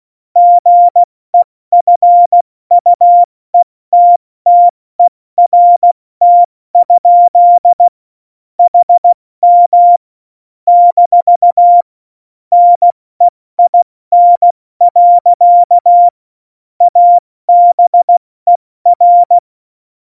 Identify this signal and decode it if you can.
Morse code, heard as a keyed tone, about 16 wpm, 700 Hz. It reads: GEFUETTERT? HM - NEIN. ABER
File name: Morsekurs / Witze / Witz 19. September 17, Tempo 70